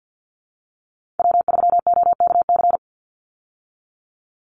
Morse code, heard as a keyed tone, about 50 wpm, 700 Hz. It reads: W3OKC
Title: Call Signs Repeated 2x - 50wpm